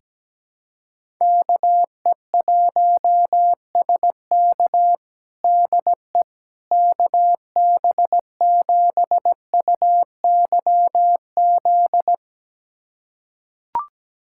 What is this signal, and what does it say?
Morse code, heard as a keyed tone, about 17 wpm, 700 Hz. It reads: KE1SK DE KB7UYZ
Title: USA Callsign DE Callsign 17wpm